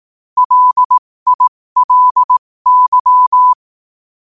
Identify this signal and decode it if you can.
Morse code, heard as a keyed tone, about 18 wpm, spaced as for 15 wpm, 1000 Hz. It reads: LILY
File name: qotdfiles / 15wpm